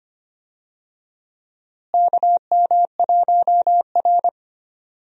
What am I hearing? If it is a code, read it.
Morse code, heard as a keyed tone, about 25 wpm, 700 Hz. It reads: KM1R
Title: USA Call Signs - 4 Characters - Repeated 2x - 25wpm